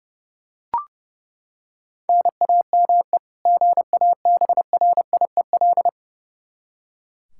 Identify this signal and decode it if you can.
Morse code, heard as a keyed tone, about 30 wpm, 700 Hz. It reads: NAME GABRIEL
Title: K1USN SST QSO Practice 30wpm